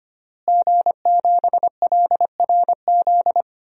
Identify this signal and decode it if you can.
Morse code, heard as a keyed tone, about 25 wpm, 700 Hz. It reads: G7LRZ
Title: DXCC-100 Call Signs - 5 Characters - Repeated 2x - 25wpm